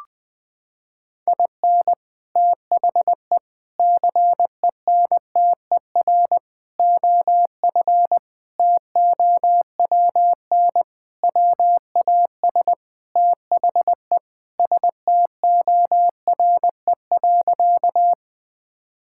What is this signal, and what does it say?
Morse code, heard as a keyed tone, about 20 wpm, 700 Hz. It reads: IN THE CENTER OF TOWN WAS THE STORE.
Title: Sentences with 4-Word Phrases - 20wpm